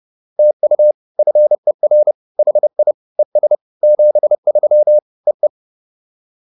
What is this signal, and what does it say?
Morse code, heard as a keyed tone, about 30 wpm, 600 Hz. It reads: TU FER HI ES 73 EE
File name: POTA208 30